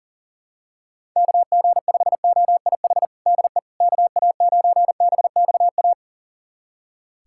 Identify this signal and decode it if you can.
Morse code, heard as a keyed tone, about 40 wpm, 700 Hz. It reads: KG5OIH DE KA9BXA
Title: USA Callsign DE Callsign - 6 Characters - 40wpm